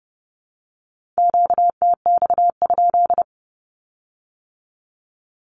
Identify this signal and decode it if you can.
Morse code, heard as a keyed tone, about 32 wpm, 700 Hz. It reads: QTX?
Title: Q-Codes with Spoken Meaning - 30wpm